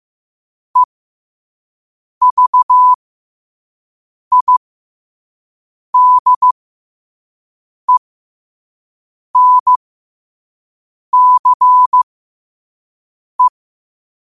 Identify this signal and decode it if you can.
Morse code, heard as a keyed tone, about 15 wpm, spaced as for 5 wpm, 1000 Hz. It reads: EVIDENCE
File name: qotdfiles / 5wpm